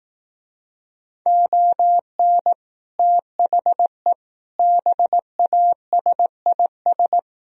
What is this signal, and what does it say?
Morse code, heard as a keyed tone, about 18 wpm, 700 Hz. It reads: ON THE BASIS
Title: Speed-Racing - Sets of 4 Words 15wpm